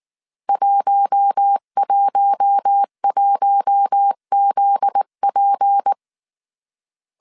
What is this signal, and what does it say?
Morse code, heard as a keyed tone, about 19 wpm, 800 Hz. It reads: 111ZP
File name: CW / DET / 1 9 1500C S B 18wpm 2551